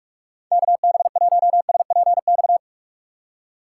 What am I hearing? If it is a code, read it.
Morse code, heard as a keyed tone, about 46 wpm, 700 Hz. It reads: KB1SPX
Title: USA Call Signs - 6 Characters - 2x Character Spacing - 45wpm